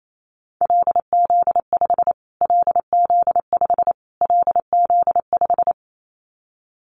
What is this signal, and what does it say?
Morse code, heard as a keyed tone, about 28 wpm, 700 Hz. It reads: LZ5 LZ5 LZ5
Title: Callsign Prefixes for Top 25 DXCC - Repeated 3x - 28wpm